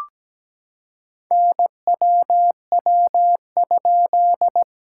Morse code, heard as a keyed tone, about 17 wpm, 700 Hz. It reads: NWW?
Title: SOTA QSO Practice 17wpm